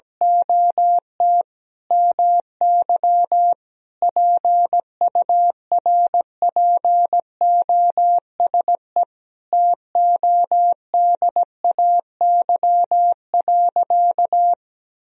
Morse code, heard as a keyed tone, about 17 wpm, 700 Hz. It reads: OT MY PURPOSE TODAY.
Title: Sentences from Top 1000 Words - Review Part 4 - 17wpm